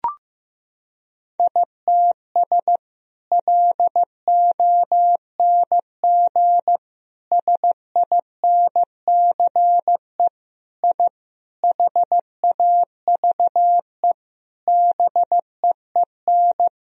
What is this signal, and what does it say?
Morse code, heard as a keyed tone, about 15 wpm, 700 Hz. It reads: ITS LONG SINCE I HAVE BEEN
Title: Sentences from Top 1000 Words - Review Part 1 - 15wpm